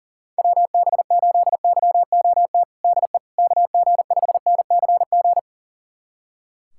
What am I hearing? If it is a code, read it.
Morse code, heard as a keyed tone, about 40 wpm, 700 Hz. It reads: WB8YOT DE KG5NCG
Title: USA Callsign DE Callsign 40wpm